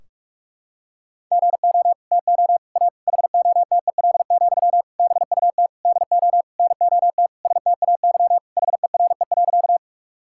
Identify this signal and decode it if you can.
Morse code, heard as a keyed tone, about 45 wpm, 700 Hz. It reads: GO TO A HOTEL, BUT DO NOT STAY HERE.
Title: Sentences from Top 1600 Words 45wpm